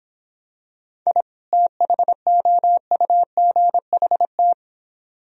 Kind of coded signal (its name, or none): Morse code